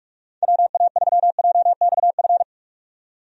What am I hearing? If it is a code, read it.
Morse code, heard as a keyed tone, about 46 wpm, 700 Hz. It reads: WA3JXF
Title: USA Call Signs - 6 Characters - 45wpm